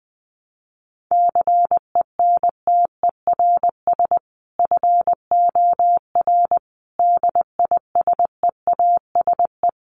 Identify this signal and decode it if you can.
Morse code, heard as a keyed tone, about 19 wpm, 700 Hz. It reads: CENTERS FOR DISEASE
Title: Sets of 5 Words Encore2 20wpm